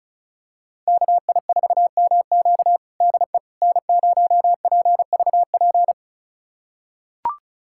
Morse code, heard as a keyed tone, about 34 wpm, 700 Hz. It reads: KI4MQ DE N0PVP
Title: USA Callsign DE Callsign - 5 Characters - 35wpm